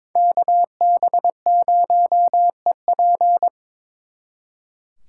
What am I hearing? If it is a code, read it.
Morse code, heard as a keyed tone, about 25 wpm, 700 Hz. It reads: KB0EP
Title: Speed Racing - USA 2x2 Call Signs 22wpm